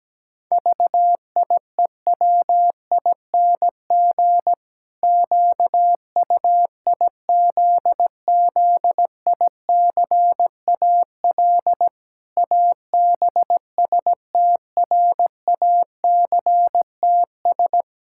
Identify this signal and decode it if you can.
Morse code, heard as a keyed tone, about 17 wpm, 700 Hz. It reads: VIEWING QUIZZICAL ABSTRACTS
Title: Pangram Challenge v2 - 17wpm